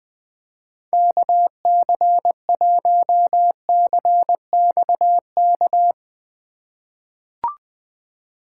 Morse code, heard as a keyed tone, about 20 wpm, 700 Hz. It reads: KC1CXK
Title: USA Call Signs - 6 Characters - 2x Character Spacing - 20wpm